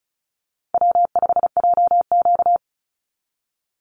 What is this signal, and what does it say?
Morse code, heard as a keyed tone, about 34 wpm, 700 Hz. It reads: W5JQ
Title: Call Signs - 35wpm